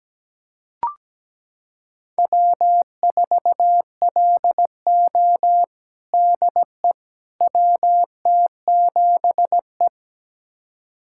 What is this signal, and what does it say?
Morse code, heard as a keyed tone, about 17 wpm, 700 Hz. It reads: W4LO DE WT7E
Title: USA Callsign DE Callsign - 4 Characters - 17wpm